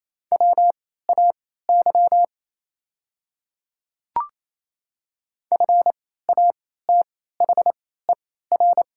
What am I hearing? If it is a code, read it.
Morse code, heard as a keyed tone, about 28 wpm, spaced as for 16 wpm, 700 Hz. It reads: WAY FATHER
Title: 3x Character Spacing - Top 300 Words - 28wpm